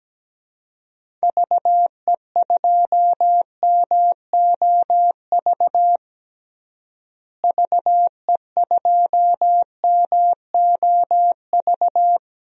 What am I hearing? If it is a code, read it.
Morse code, heard as a keyed tone, about 17 wpm, 700 Hz. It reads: VE2MOV VE2MOV
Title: DXCC-100 Call Signs - 6 Characters - Repeated 3x - 17wpm